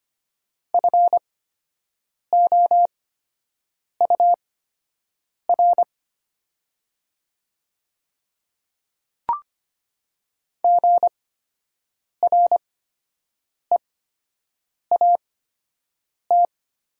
Morse code, heard as a keyed tone, about 25 wpm, spaced as for 7 wpm, 700 Hz. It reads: FOUR GREAT
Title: 8x Character Spacing - Top 300 Words - 25wpm